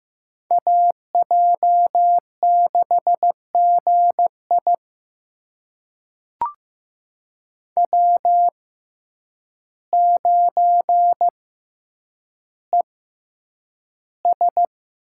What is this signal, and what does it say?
Morse code, heard as a keyed tone, about 15 wpm, spaced as for 8 wpm, 700 Hz. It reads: AJ6GI W9ES
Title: USA Call Signs - 5 Characters - 6x Character Spacing - 15wpm